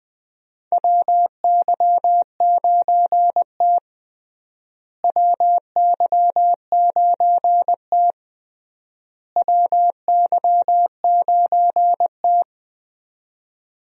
Morse code, heard as a keyed tone, about 20 wpm, 700 Hz. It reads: WY9T WY9T WY9T
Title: USA Call Signs - 4 Characters - Repeated 3x - 20wpm